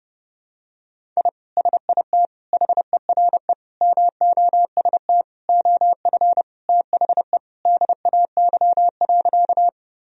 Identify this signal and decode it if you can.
Morse code, heard as a keyed tone, about 30 wpm, 700 Hz. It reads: I SIT HERE MOST OF THE DAY.
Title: Sentences from Top 1000 Words - Review Part 1 - 30wpm